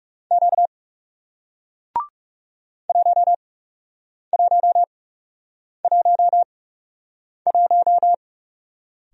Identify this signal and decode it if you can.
Morse code, heard as a keyed tone, about 41 wpm, spaced as for 8 wpm, 700 Hz. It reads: Q 1111